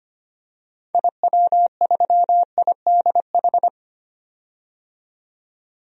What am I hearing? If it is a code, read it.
Morse code, heard as a keyed tone, about 25 wpm, 700 Hz. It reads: IW3IDH
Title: DXCC-100 Call Signs - 6 Characters - 25wpm